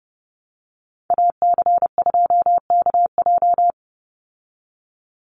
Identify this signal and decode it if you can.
Morse code, heard as a keyed tone, about 30 wpm, 700 Hz. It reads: AC2KJ
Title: Call Signs Repeated 2x - 30wpm